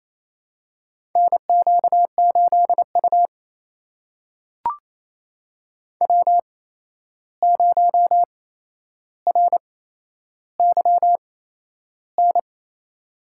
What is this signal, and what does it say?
Morse code, heard as a keyed tone, about 28 wpm, spaced as for 11 wpm, 700 Hz. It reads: NQ8U W0RYN
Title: USA Call Signs - 8x Character Spacing - 28wpm